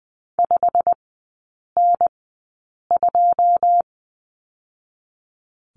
Morse code, heard as a keyed tone, about 20 wpm, spaced as for 8 wpm, 700 Hz. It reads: 5N2
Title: 152 Lesson 25e - Letter p - Three Characters 20wpm